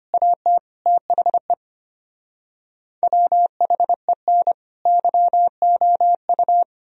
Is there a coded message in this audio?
Yes: Morse code from 0.1 to 6.6 s, heard as a keyed tone, about 33 wpm, 700 Hz. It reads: AT THE WHEN YOU